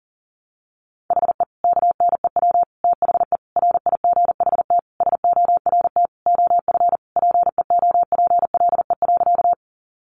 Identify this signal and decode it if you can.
Morse code, heard as a keyed tone, about 40 wpm, 700 Hz. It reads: HE KNEW THE RIGHT SORT OF PEOPLE.
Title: Sentences from Top 1300 Words 40wpm